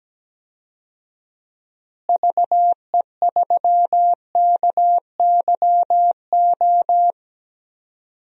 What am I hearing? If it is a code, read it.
Morse code, heard as a keyed tone, about 17 wpm, 700 Hz. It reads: VE3KYO
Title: DXCC-100 Call Signs - 6 Characters - Repeated 2x - 17wpm